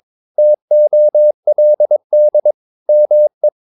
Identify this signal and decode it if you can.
Morse code, heard as a keyed tone, about 23 wpm, 600 Hz. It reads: TOLD ME
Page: Romanen Never af Ken Follet på engelsk.